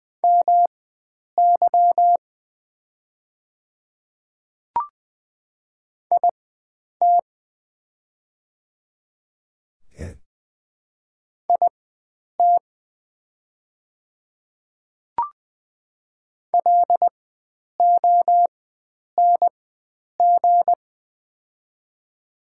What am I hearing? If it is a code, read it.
Morse code, heard as a keyed tone, about 20 wpm, spaced as for 9 wpm, 700 Hz. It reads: MY IT IT LONG
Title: 4x Character Spacing - Top 100 Words - 20wpm